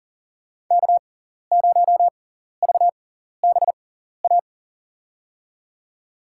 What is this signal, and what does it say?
Morse code, heard as a keyed tone, about 40 wpm, spaced as for 14 wpm, 700 Hz. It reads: K0VBA
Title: USA Call Signs - 5 Characters - 6x Character Spacing - 40wpm